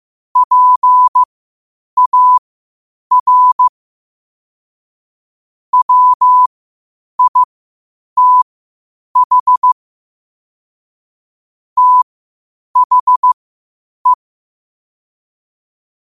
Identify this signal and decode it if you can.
Morse code, heard as a keyed tone, about 15 wpm, spaced as for 8 wpm, 1000 Hz. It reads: PAR WITH THE
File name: qotdfiles / 8wpm